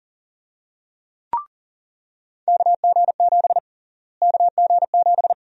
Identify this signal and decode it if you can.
Morse code, heard as a keyed tone, about 39 wpm, 700 Hz. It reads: KG7 KG7
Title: USA Call Sign Prefixes - Repeated 2x - 40wpm